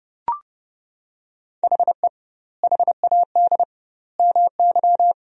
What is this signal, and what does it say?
Morse code, heard as a keyed tone, about 30 wpm, 700 Hz. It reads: HE HAD MY
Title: 2x Word-Spacing - Sentences from Top 400 Words - 30wpm